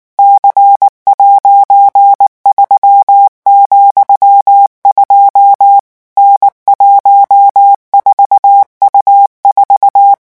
Morse code, heard as a keyed tone, about 19 wpm, 800 Hz. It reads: C'3,2 N14U4
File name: morse mp3 / revision13-vitesse 18 mots